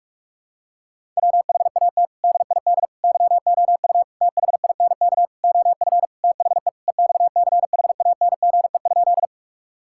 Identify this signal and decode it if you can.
Morse code, heard as a keyed tone, about 46 wpm, 700 Hz. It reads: WHAT DID YOU THINK OF THE EXCHANGE?
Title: Sentences from Top 1000 Words - Review Part 6 - 45wpm